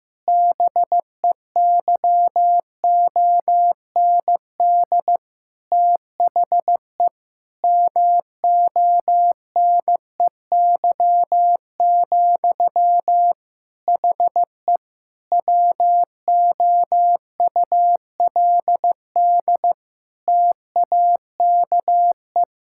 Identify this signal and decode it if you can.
Morse code, heard as a keyed tone, about 15 wpm, 700 Hz. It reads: BEYOND THE MONEY, HE WOULD TAKE
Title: Sentences from Top 1000 Words - Review Part 6 - 15wpm